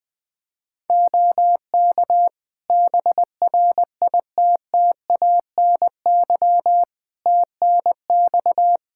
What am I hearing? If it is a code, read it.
Morse code, heard as a keyed tone, about 20 wpm, 700 Hz. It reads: OK BRITTANY TNX
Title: Practice QSO - Exchange 4 - Ending - 20wpm